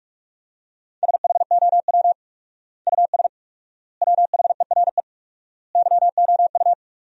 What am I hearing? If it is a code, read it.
Morse code, heard as a keyed tone, about 46 wpm, 700 Hz. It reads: SHOW US WHERE YOU
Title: 4x Word-Spacing - Sentences from Top 200 Words - 45wpm